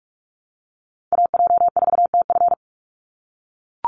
Morse code, heard as a keyed tone, about 44 wpm, 700 Hz. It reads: AJ4TF
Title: USA Call Signs 45wpm